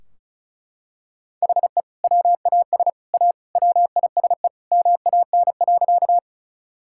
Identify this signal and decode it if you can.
Morse code, heard as a keyed tone, about 34 wpm, 700 Hz. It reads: HE WAS A WISE MAN.
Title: Sentences from Top 1600 Words 35wpm